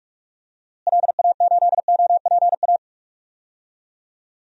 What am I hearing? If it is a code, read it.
Morse code, heard as a keyed tone, about 42 wpm, 700 Hz. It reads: LA8OPA
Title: DXCC-100 Call Signs - 6 Characters - Repeated 2x - 45wpm